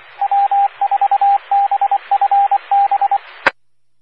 Morse code, heard as a keyed tone, about 24 wpm, 800 Hz, with radio noise. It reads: W4BFB
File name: W4BFB Charlotte Regional Skywarn Repeater